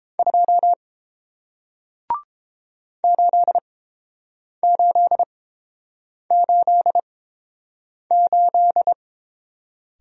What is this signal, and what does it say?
Morse code, heard as a keyed tone, about 29 wpm, spaced as for 7 wpm, 700 Hz. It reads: 2 8888